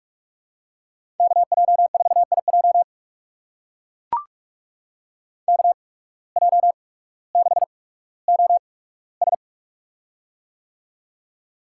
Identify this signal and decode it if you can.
Morse code, heard as a keyed tone, about 44 wpm, spaced as for 19 wpm, 690 Hz. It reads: KJ4IJ KJ6OS
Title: USA Call Signs - 5 Characters - 8x Character Spacing - 45wpm